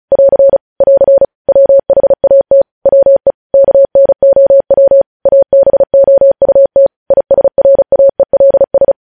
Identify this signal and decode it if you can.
Morse code, heard as a keyed tone, about 36 wpm, 550 Hz. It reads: <AR> <AR> WHAT WE KNOW ABOUT ISRAELS
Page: news headlines in morse code at 35 wpm